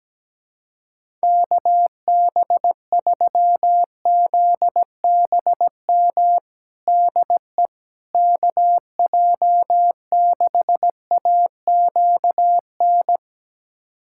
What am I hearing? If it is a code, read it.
Morse code, heard as a keyed tone, about 17 wpm, 700 Hz. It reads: KB3ZBM DE KJ6AQN
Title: USA Callsign DE Callsign - 6 Characters - 17wpm